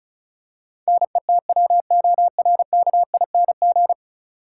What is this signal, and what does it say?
Morse code, heard as a keyed tone, about 36 wpm, 700 Hz. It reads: NETWORKING
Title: Practice QSO - Exchange 3 - More Info - 35wpm